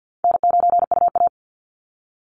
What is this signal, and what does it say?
Morse code, heard as a keyed tone, about 48 wpm, 700 Hz. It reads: N9UA
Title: Call Signs Repeated 3x - 50wpm